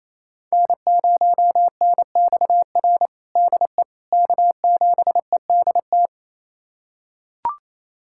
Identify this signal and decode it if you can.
Morse code, heard as a keyed tone, about 28 wpm, 700 Hz. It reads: N0NXR DE K7EDT
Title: USA Callsign DE Callsign - 5 Characters - 28wpm